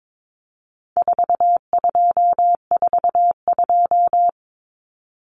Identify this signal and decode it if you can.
Morse code, heard as a keyed tone, about 22 wpm, 700 Hz. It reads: 4242
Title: Numbers - 4-Digit 22wpm